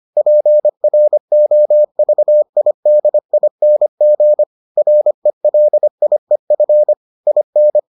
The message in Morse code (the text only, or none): PROVIDING RELIEF IN